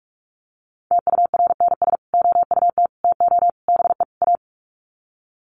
Morse code, heard as a keyed tone, about 44 wpm, 700 Hz. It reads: TURNS OUT TO BE A
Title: Sets of 5 Words Encore 45wpm